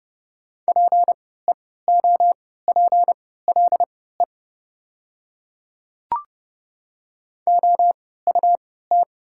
Morse code, heard as a keyed tone, about 30 wpm, spaced as for 17 wpm, 700 Hz. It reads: PEOPLE OUT
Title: 3x Character Spacing - Top 200 Words - 30wpm